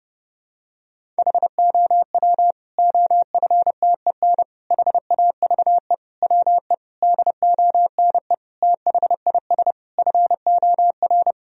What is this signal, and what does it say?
Morse code, heard as a keyed tone, about 32 wpm, 700 Hz. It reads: HOW OFTEN HAVE WE DONE THIS FOR
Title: Sentences from Top 1000 Words - Review Part 1 - 30wpm